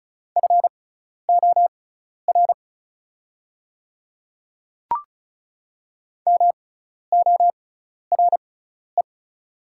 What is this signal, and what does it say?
Morse code, heard as a keyed tone, about 34 wpm, spaced as for 12 wpm, 690 Hz. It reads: FOR MORE
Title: 6x Character Spacing - Top 300 Words - 35wpm